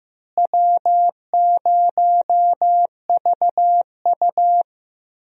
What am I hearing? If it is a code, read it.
Morse code, heard as a keyed tone, about 15 wpm, 700 Hz. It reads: W0VU
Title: USA Call Signs - 4 Characters - Repeated 2x - 15wpm